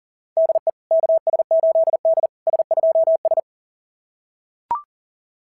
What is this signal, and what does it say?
Morse code, heard as a keyed tone, about 39 wpm, 690 Hz. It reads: DE KS8D S2S
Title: SOTA QSO Practice 40wpm